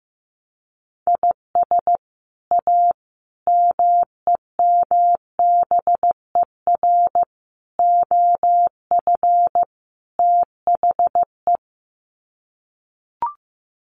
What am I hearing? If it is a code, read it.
Morse code, heard as a keyed tone, about 15 wpm, 700 Hz. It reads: IS A MEMBER OF THE
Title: Sets of 5 Words 15wpm